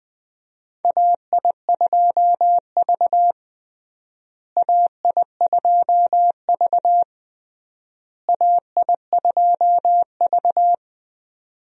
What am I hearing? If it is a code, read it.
Morse code, heard as a keyed tone, about 20 wpm, 700 Hz. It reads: AI2V AI2V AI2V
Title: USA Call Signs - 4 Characters - Repeated 3x - 20wpm